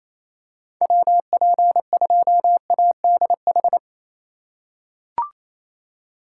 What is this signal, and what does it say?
Morse code, heard as a keyed tone, about 28 wpm, 700 Hz. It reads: WP2ADH
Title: USA Call Signs - 6 Characters - 4x Character Spacing - 28wpm